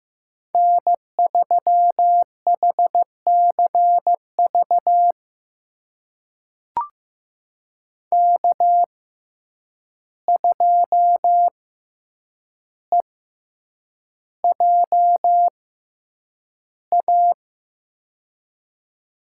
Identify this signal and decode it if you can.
Morse code, heard as a keyed tone, about 15 wpm, spaced as for 8 wpm, 700 Hz. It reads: N3HCV K2EJA
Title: USA Call Signs - 5 Characters - 6x Character Spacing - 15wpm